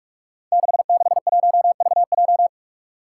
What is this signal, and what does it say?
Morse code, heard as a keyed tone, about 44 wpm, 700 Hz. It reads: 661VJ
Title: Five Challenging Letter-Number Combinations 45wpm